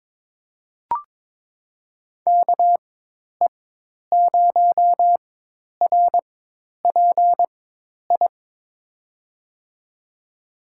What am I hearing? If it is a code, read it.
Morse code, heard as a keyed tone, about 22 wpm, spaced as for 10 wpm, 700 Hz. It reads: KE0RPI
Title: USA Call Signs - 6 Characters - 4x Character Spacing - 22wpm